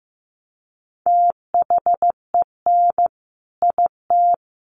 Morse code, heard as a keyed tone, about 15 wpm, 700 Hz. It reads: THEN IT
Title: A Sound Of Thunder 15wpm